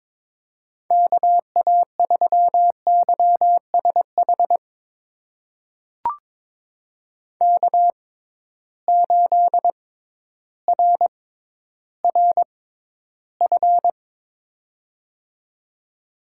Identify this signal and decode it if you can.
Morse code, heard as a keyed tone, about 22 wpm, spaced as for 12 wpm, 700 Hz. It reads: KA3YSH K8RRF
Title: USA Call Signs - 6x Character Spacing - 22wpm